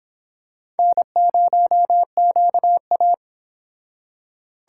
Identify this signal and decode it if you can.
Morse code, heard as a keyed tone, about 26 wpm, 700 Hz. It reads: N0QA